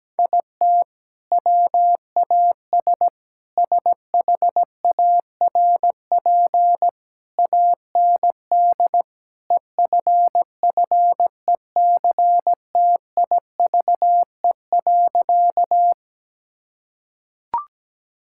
Morse code, heard as a keyed tone, about 17 wpm, 700 Hz. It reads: IT WAS SHARP AND EFFECTIVE.
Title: Sentences from Top 1600 Words 17wpm